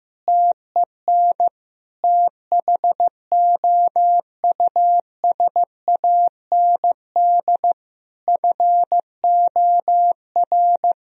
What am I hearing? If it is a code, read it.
Morse code, heard as a keyed tone, about 15 wpm, 700 Hz. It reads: TEN THOUSAND FOR
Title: Sentences from Top 1000 Words - Review Part 3 - 15wpm